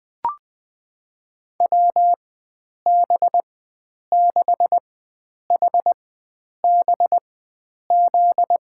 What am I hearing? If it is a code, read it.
Morse code, heard as a keyed tone, about 21 wpm, spaced as for 9 wpm, 700 Hz. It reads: WB6HBZ
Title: USA Call Signs - 6 Characters - 4x Character Spacing - 20wpm